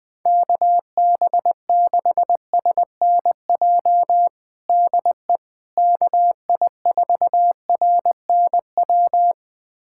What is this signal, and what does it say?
Morse code, heard as a keyed tone, about 20 wpm, 700 Hz. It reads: KB6SNJ DE KI4RNW
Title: USA Callsign DE Callsign - 6 Characters - 20wpm